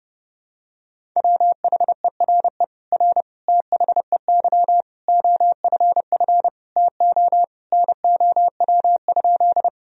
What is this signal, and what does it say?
Morse code, heard as a keyed tone, about 32 wpm, 700 Hz. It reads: WHERE R THEY OFF TO NOW?
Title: Sentences from Top 200 Words and Abbreviations 30wpm